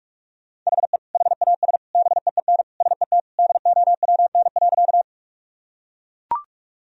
Morse code, heard as a keyed tone, about 42 wpm, 700 Hz. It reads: HE HAS BEEN SET DOWN.
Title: Sentences from Top 1000 Words - Review Part 1 - 45wpm